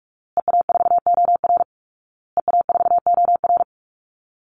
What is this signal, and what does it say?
Morse code, heard as a keyed tone, about 44 wpm, 700 Hz. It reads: EA4OR EA4OR
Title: Call Signs Repeated 2x - 45wpm